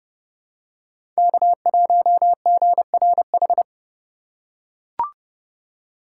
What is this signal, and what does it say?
Morse code, heard as a keyed tone, about 30 wpm, 700 Hz. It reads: K1GRH
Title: USA Call Signs - 5 Characters - 2x Character Spacing - 30wpm